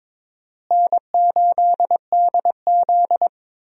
Morse code, heard as a keyed tone, about 22 wpm, 700 Hz. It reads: N8DZ
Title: USA Call Signs - 4 Characters - Repeated 3x - 22wpm